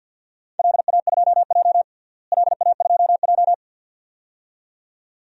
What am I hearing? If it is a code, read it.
Morse code, heard as a keyed tone, about 50 wpm, 700 Hz. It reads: LA2J LA2J
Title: DXCC-100 Call Signs - 4 Characters - Repeated 2x - 50wpm